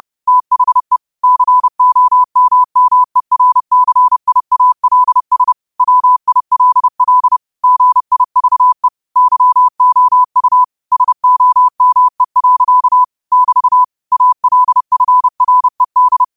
Morse code, heard as a keyed tone, about 30 wpm, 1000 Hz. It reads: THE COMMERCIALS WILL GIVE YOU SOME. <BT> ALFRED